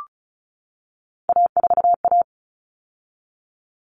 Morse code, heard as a keyed tone, about 34 wpm, 700 Hz. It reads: A4A
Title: All Cut Numbers - 3-Digit 35wpm